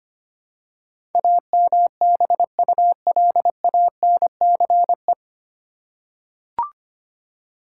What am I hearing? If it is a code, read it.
Morse code, heard as a keyed tone, about 25 wpm, 700 Hz. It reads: AMBULANCE
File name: Nine Letter Words - 25wpm